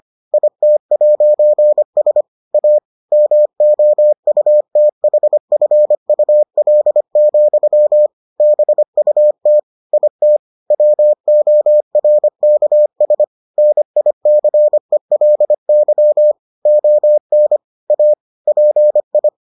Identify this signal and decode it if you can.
Morse code, heard as a keyed tone, about 25 wpm, 600 Hz. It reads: IT'S A MOUTHFUL, BUT IT WORKS NICELY ON A PI